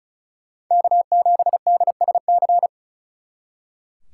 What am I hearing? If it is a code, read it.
Morse code, heard as a keyed tone, about 46 wpm, 700 Hz. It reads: K7DSC